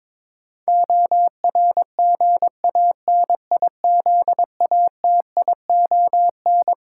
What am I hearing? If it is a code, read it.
Morse code, heard as a keyed tone, about 22 wpm, 700 Hz. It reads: ORGANIZATION
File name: practice / 22wpm / Twelve Letter Words - 22wpm